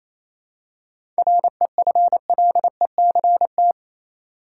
Morse code, heard as a keyed tone, about 28 wpm, 700 Hz. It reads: REFLECT
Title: Seven Letter Words - 28wpm